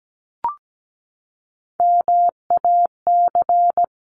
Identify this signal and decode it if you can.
Morse code, heard as a keyed tone, about 17 wpm, 700 Hz. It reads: MAC
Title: CWT Exchange 17wpm